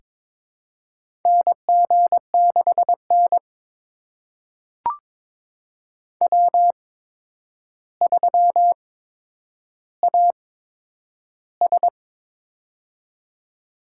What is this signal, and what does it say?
Morse code, heard as a keyed tone, about 22 wpm, spaced as for 9 wpm, 700 Hz. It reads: NG6N W3AS
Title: USA Call Signs - 4 Characters - 8x Character Spacing - 22wpm